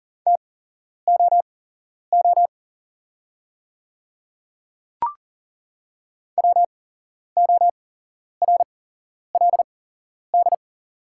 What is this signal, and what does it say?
Morse code, heard as a keyed tone, about 40 wpm, spaced as for 11 wpm, 700 Hz. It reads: TOO WORLD
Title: 8x Character Spacing - Top 200 Words - 40wpm